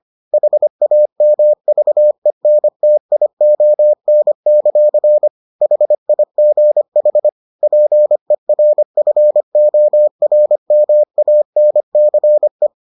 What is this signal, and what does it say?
Morse code, heard as a keyed tone, about 25 wpm, 600 Hz. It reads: HAMVENTION; HIGH PERFORMANCE